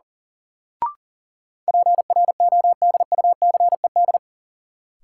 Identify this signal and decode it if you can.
Morse code, heard as a keyed tone, about 37 wpm, 700 Hz. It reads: PRODUCED
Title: Sentences from Top 1000 Words - Review Part 6 - New Word Drill - 40wpm